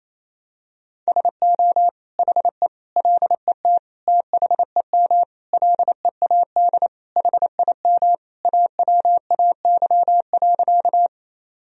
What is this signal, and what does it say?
Morse code, heard as a keyed tone, about 28 wpm, 700 Hz. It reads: SO HE LET THEM LEAD HIM AWAY.
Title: Sentences from Top 1000 Words - Review Part 2 - 28wpm